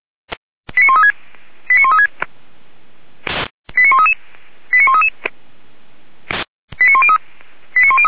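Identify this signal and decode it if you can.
Five-tone selective call (ZVEI), heard as quick runs of five tones, tones of about 70 ms. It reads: 89136 89136 89133 89133 89103
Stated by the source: ZVEI five-tone call received by SDR, archived by Feuerwehr Orlamünde.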